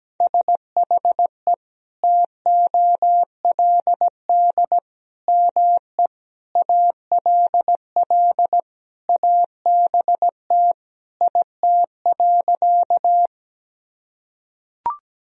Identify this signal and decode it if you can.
Morse code, heard as a keyed tone, about 17 wpm, 700 Hz. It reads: SHE TOLD ME ALL ABT IT.
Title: Sentences from Top 400 Words and Abbreviations 17wpm